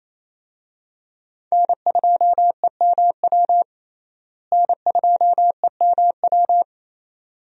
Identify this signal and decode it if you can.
Morse code, heard as a keyed tone, about 28 wpm, 700 Hz. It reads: N2EMW N2EMW
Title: USA Call Signs - 5 Characters - Repeated 2x - 28wpm